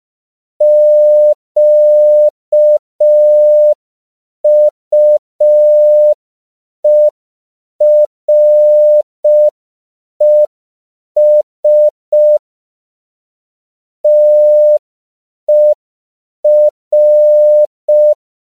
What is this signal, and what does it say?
Morse code, heard as a keyed tone, about 5 wpm, 600 Hz. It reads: QUERES TER
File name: CW / 5PPM